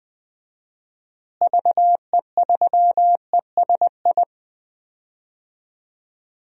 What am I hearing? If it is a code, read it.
Morse code, heard as a keyed tone, about 20 wpm, 700 Hz. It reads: VE3ESI
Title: DXCC-100 Call Signs - 6 Characters - Repeated 2x - 20wpm